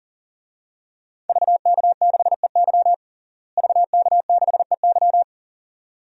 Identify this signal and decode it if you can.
Morse code, heard as a keyed tone, about 37 wpm, 700 Hz. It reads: VK6EY VK6EY
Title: DXCC-100 Call Signs - 5 Characters - Repeated 2x - 40wpm